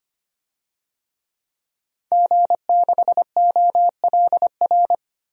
Morse code, heard as a keyed tone, about 25 wpm, 700 Hz. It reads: G6OLR
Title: DXCC-100 Call Signs - 5 Characters - 25wpm